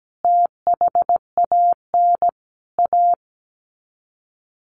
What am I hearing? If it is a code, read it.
Morse code, heard as a keyed tone, about 17 wpm, 700 Hz. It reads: THAN A
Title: Sets of 4 Words 17wpm